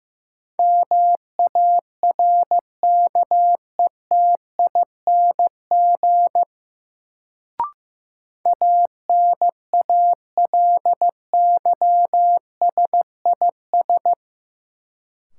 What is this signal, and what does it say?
Morse code, heard as a keyed tone, about 16 wpm, 700 Hz. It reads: MARKETING ANALYSIS
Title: Sentences from Top 1000 Words - Review Part 5 - New Word Drill - 15wpm